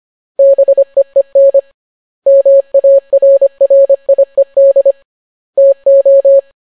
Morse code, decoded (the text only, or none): BEEN MARRIED TO